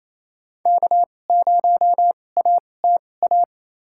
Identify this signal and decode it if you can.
Morse code, heard as a keyed tone, about 28 wpm, spaced as for 20 wpm, 700 Hz. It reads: K0ATA
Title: USA Call Signs - 5 Characters - 2x Character Spacing - 28wpm